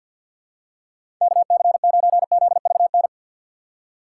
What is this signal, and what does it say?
Morse code, heard as a keyed tone, about 52 wpm, 700 Hz. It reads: KC9ZVN
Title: USA Call Signs - 6x Character Spacing - 50wpm